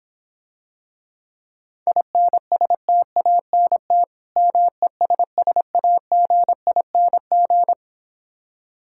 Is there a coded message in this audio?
Yes: Morse code from 1.9 to 7.7 s, heard as a keyed tone, about 26 wpm, 700 Hz. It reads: INSTANT MESSAGING